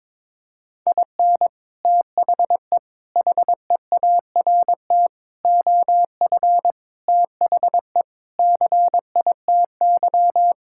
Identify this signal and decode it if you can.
Morse code, heard as a keyed tone, about 22 wpm, 700 Hz. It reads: IN THE HEART OF THE CITY
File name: practice / 22wpm / Sentences with 3-Word Phrases - 22wpm